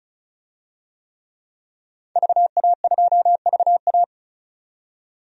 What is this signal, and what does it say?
Morse code, heard as a keyed tone, about 34 wpm, 700 Hz. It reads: VA2VA
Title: DXCC-100 Call Signs - 5 Characters - 35wpm